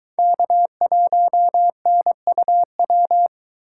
Morse code, heard as a keyed tone, about 22 wpm, 700 Hz. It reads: K1NUW